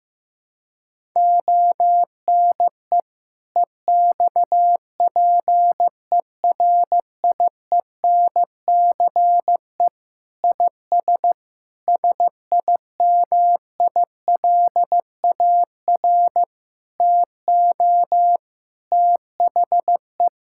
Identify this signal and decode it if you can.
Morse code, heard as a keyed tone, about 15 wpm, 700 Hz. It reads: ONE EXPERIENCE IS SIMILAR TO THE
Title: Sentences from Top 1000 Words - Review Part 5 - 15wpm